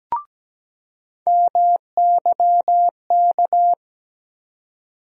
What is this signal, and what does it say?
Morse code, heard as a keyed tone, about 17 wpm, 700 Hz. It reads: MYK
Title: USA Call Sign Postfixes - 3 Letters 17wpm